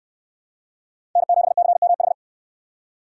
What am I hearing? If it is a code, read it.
Morse code, heard as a keyed tone, about 60 wpm, 700 Hz. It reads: N7CMH